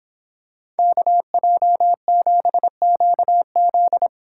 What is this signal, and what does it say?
Morse code, heard as a keyed tone, about 26 wpm, 700 Hz. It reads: KJ7QZ